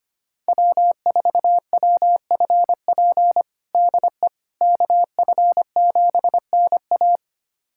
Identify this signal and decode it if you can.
Morse code, heard as a keyed tone, about 25 wpm, 700 Hz. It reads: W4WFP DE KF7NA
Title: USA Callsign DE Callsign - 5 Characters - 25wpm